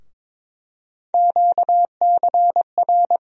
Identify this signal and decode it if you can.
Morse code, heard as a keyed tone, about 22 wpm, 700 Hz. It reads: QCR
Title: USA Call Sign Postfixes - 3 Letters 22wpm